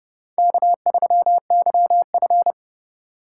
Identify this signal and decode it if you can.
Morse code, heard as a keyed tone, about 29 wpm, 700 Hz. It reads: K3YF